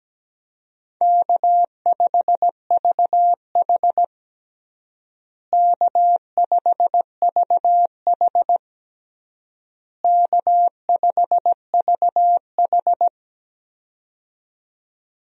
Morse code, heard as a keyed tone, about 17 wpm, 700 Hz. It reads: K5VH K5VH K5VH
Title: USA Call Signs - 4 Characters - Repeated 3x - 17wpm